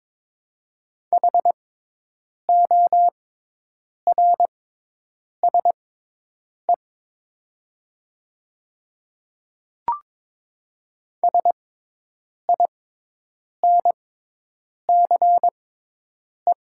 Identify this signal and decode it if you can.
Morse code, heard as a keyed tone, about 22 wpm, spaced as for 8 wpm, 700 Hz. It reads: HORSE SINCE
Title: 6x Character Spacing - Top 300 Words - 22wpm